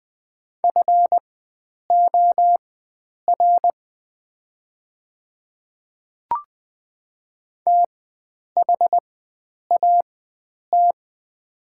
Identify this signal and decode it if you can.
Morse code, heard as a keyed tone, about 20 wpm, spaced as for 9 wpm, 700 Hz. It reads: FOR THAT
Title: 4x Character Spacing - Top 100 Words - 20wpm